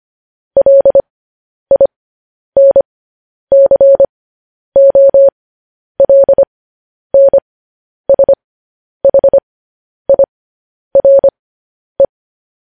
Morse code, heard as a keyed tone, about 25 wpm, spaced as for 10 wpm, 550 Hz. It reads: LINCOLNSHIRE
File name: podcast / News headlines-1x-25x10